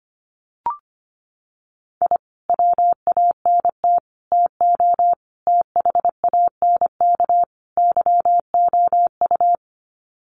Morse code, heard as a keyed tone, about 25 wpm, 700 Hz. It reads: I WANT TO THANK YOU
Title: Sets of 5 Words Encore 25wpm